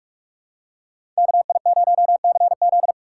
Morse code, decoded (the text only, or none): KI0CZ